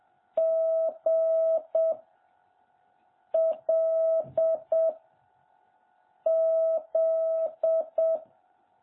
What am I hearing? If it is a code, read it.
Morse code, heard as a keyed tone, about 7 wpm, 660 Hz. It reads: G L Z
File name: jh8cbh / 2021cw